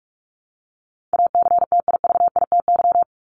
Morse code, heard as a keyed tone, about 44 wpm, 700 Hz. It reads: ACTIVITY
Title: ES:S Suffix - Top 100 Words with Base Words 45wpm